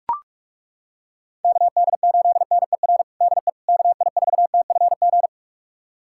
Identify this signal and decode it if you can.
Morse code, heard as a keyed tone, about 42 wpm, 700 Hz. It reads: KD8NER DE KI4TFG
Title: USA Callsign DE Callsign - 6 Characters - 45wpm